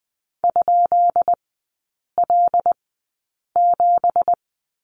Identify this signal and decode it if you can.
Morse code, heard as a keyed tone, about 20 wpm, spaced as for 8 wpm, 700 Hz. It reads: ?L7
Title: 198 Lesson 32e - Letter k - Three Characters 20wpm